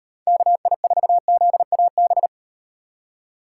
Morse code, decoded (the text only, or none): KI4ZAB